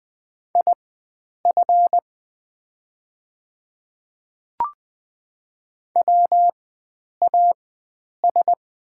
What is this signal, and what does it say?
Morse code, heard as a keyed tone, about 20 wpm, spaced as for 9 wpm, 690 Hz. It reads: IF WAS
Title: 4x Character Spacing - Top 100 Words - 20wpm